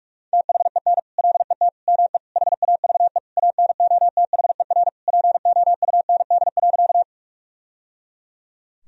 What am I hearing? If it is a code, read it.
Morse code, heard as a keyed tone, about 46 wpm, 700 Hz. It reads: THEN LET ME HAVE ANOTHER POUND.
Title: Sentences from Top 1000 Words - Review Part 2 - 45wpm